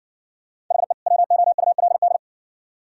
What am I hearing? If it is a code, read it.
Morse code, heard as a keyed tone, about 60 wpm, 700 Hz. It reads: HE WOULD